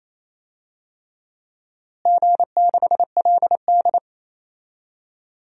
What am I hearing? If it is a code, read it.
Morse code, heard as a keyed tone, about 28 wpm, 700 Hz. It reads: G6LD
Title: DXCC-100 Call Signs - 4 Characters - Repeated 3x - 28wpm